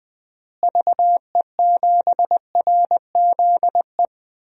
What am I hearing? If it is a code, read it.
Morse code, heard as a keyed tone, about 20 wpm, 700 Hz. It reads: VE7RZE
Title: DXCC-100 Call Signs - 6 Characters - Repeated 3x - 20wpm